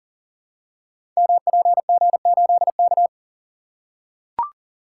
Morse code, heard as a keyed tone, about 41 wpm, 700 Hz. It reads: MPG8K
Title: Code Groups - 5 Letter:Numbers - 40wpm